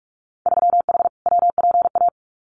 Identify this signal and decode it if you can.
Morse code, heard as a keyed tone, about 44 wpm, 700 Hz. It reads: 3H WPA
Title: Winter Field Day Exchanges Encore 45wpm